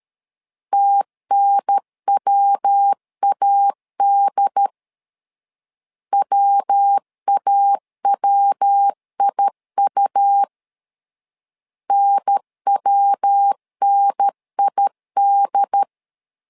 Morse code, heard as a keyed tone, about 13 wpm, 800 Hz. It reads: TNWAD WAWIU NWNID
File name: LEZIONI / 3 MWRO / 3 MW 12wpm 1633